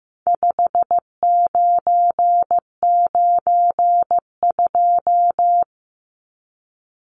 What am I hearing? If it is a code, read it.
Morse code, heard as a keyed tone, about 15 wpm, 700 Hz. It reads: 5992
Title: Numbers - 4-Digit 15wpm